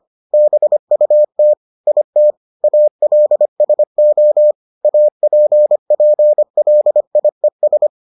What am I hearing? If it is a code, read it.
Morse code, heard as a keyed tone, about 25 wpm, 600 Hz. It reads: BUT IT ALSO APPLIES